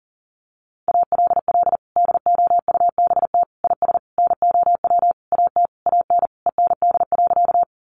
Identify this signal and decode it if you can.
Morse code, heard as a keyed tone, about 40 wpm, 700 Hz. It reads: ALL DOUBT IS NOW AT AN END.
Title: Sentences from Top 1500 Words 40wpm